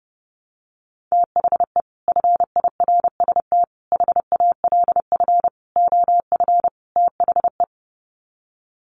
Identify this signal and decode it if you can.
Morse code, heard as a keyed tone, about 30 wpm, 700 Hz. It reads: THE FIRST HALF OF THE
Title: Sets of 5 Words 30wpm